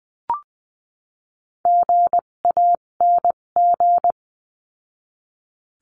Morse code, heard as a keyed tone, about 20 wpm, spaced as for 17 wpm, 700 Hz. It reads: GANG
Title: 257 Lesson 42b - Words 14-20wpm